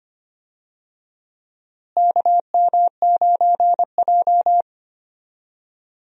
Morse code, heard as a keyed tone, about 25 wpm, 700 Hz. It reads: KM9J
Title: USA Call Signs - 4 Characters - Repeated 3x - 25wpm